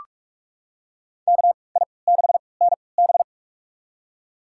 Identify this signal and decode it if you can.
Morse code, heard as a keyed tone, about 47 wpm, spaced as for 26 wpm, 700 Hz. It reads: KI6NB
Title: USA Call Signs - 3x Character Spacing - 45wpm